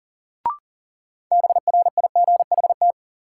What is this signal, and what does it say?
Morse code, heard as a keyed tone, about 40 wpm, 700 Hz. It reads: BRIGHT
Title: Sentences from Top 1000 Words - New Word Drill - 40wpm